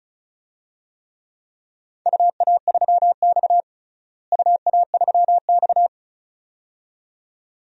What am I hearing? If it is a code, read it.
Morse code, heard as a keyed tone, about 36 wpm, 700 Hz. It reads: UA3X UA3X
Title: DXCC-100 Call Signs - 4 Characters - Repeated 2x - 35wpm